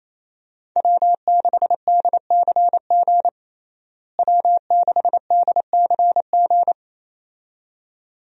Morse code, heard as a keyed tone, about 28 wpm, 700 Hz. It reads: W6DCG W6DCG
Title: USA Call Signs - 5 Characters - Repeated 2x - 28wpm